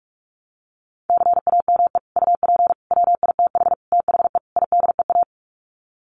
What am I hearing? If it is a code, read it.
Morse code, heard as a keyed tone, about 44 wpm, 700 Hz. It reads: CAME UP WITH THE IDEA
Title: Sets of 5 Words Encore2 45wpm